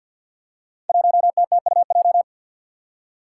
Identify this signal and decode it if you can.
Morse code, heard as a keyed tone, about 47 wpm, 700 Hz. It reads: 1TTUJ
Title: Five Challenging Letter-Number Combinations 50wpm